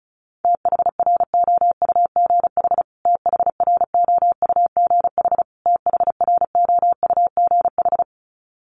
Morse code, heard as a keyed tone, about 34 wpm, 700 Hz. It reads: THROUGH THROUGH THROUGH
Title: Top 500 Words Repeated 35wpm